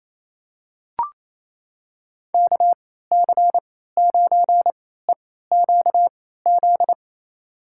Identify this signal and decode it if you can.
Morse code, heard as a keyed tone, about 28 wpm, spaced as for 16 wpm, 700 Hz. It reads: KC9EQZ
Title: USA Call Signs - 3x Character Spacing - 28wpm